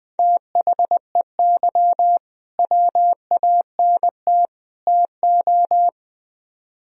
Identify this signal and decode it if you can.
Morse code, heard as a keyed tone, about 20 wpm, 700 Hz. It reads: THEY WANT TO
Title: Speed-Racing - Sets of 3 Words 17wpm